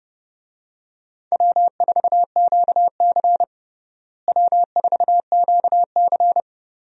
Morse code, heard as a keyed tone, about 30 wpm, 700 Hz. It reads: W4QC W4QC
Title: USA Call Signs - 4 Characters - Repeated 2x - 30wpm